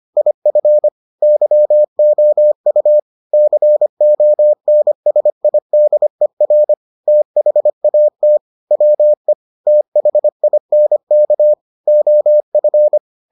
Morse code, heard as a keyed tone, about 25 wpm, 600 Hz. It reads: IF YOU CONSIDER THAT WE THINK OF